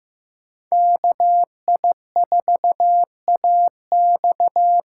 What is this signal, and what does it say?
Morse code, heard as a keyed tone, about 15 wpm, 700 Hz. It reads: KI4AX
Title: USA Call Signs - 6 Characters - 8x Character Spacing - 15wpm